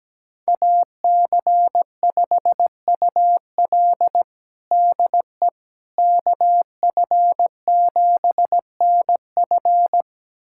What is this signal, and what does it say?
Morse code, heard as a keyed tone, about 17 wpm, 700 Hz. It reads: AC5UL DE KF7NF
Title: USA Callsign DE Callsign 17wpm